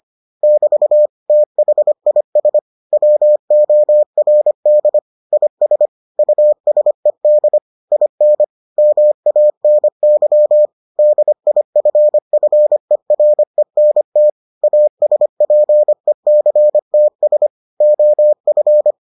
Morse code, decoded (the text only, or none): <BT> THIS WORD IS USED IN MANY DIFFERENT ASPECTS OF